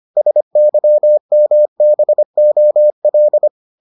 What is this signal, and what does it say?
Morse code, heard as a keyed tone, about 25 wpm, 600 Hz. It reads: SYMBOL